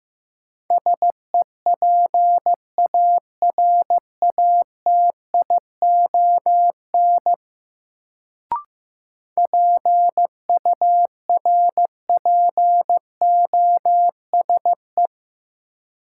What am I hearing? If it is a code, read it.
Morse code, heard as a keyed tone, about 15 wpm, 700 Hz. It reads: SEPARATION PURPOSE
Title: Sets of 5 Words - New Word Drill - 15wpm